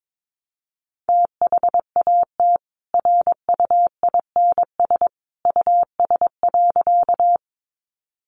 Morse code, heard as a keyed tone, about 22 wpm, 700 Hz. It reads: THAT RUINS US.
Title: A Sound Of Thunder 22wpm